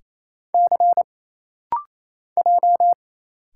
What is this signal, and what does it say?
Morse code, heard as a keyed tone, about 27 wpm, 700 Hz. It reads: C J
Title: Single Letters - Rapid-Fire 28wpm